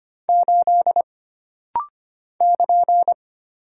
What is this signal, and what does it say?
Morse code, heard as a keyed tone, about 25 wpm, 700 Hz. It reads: 8 <KN>
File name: Single Character (Letter:Number:Punctuation:Prosign) - Rapid-Fire - 25wpm